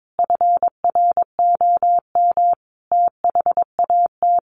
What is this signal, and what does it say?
Morse code, heard as a keyed tone, about 22 wpm, 700 Hz. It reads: FROM THAT
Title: Sentences from Top 1400 Words 22wpm